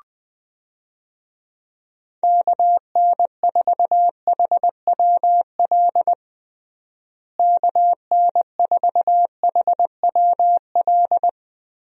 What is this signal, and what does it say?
Morse code, heard as a keyed tone, about 21 wpm, 700 Hz. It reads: KN4HWL KN4HWL
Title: USA Call Signs - 6 Characters - Repeated 3x - 20wpm